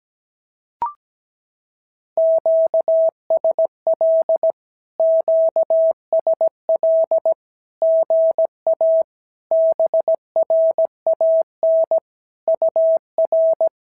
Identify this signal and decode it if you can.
Morse code, heard as a keyed tone, about 17 wpm, 650 Hz. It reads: QSL QSL GA BRAN UR
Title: SOTA QSO Practice 17wpm